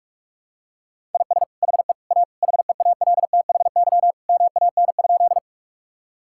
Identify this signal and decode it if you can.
Morse code, heard as a keyed tone, about 46 wpm, 700 Hz. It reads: IS HE A HEALTHY MAN?
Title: Sentences from Top 1000 Words - Review Part 5 - 45wpm